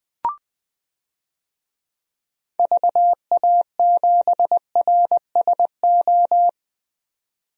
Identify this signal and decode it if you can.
Morse code, heard as a keyed tone, about 20 wpm, 700 Hz. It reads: VA7RSO
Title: DXCC-100 Call Signs - 6 Characters - 20wpm